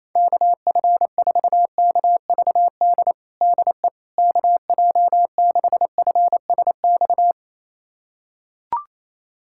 Morse code, heard as a keyed tone, about 28 wpm, 700 Hz. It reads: KF4KVD DE KJ6FSX
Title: USA Callsign DE Callsign - 6 Characters - 28wpm